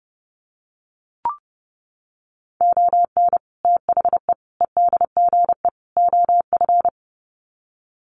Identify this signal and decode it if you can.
Morse code, heard as a keyed tone, about 30 wpm, 700 Hz. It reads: ON THE EDGE OF
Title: Sets of 4 Words 30wpm